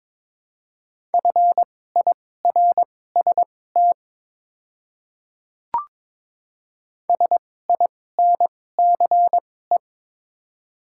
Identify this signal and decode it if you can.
Morse code, heard as a keyed tone, about 22 wpm, spaced as for 16 wpm, 700 Hz. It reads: FIRST SINCE
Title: 2x Character Spacing - Top 300 Words - 22wpm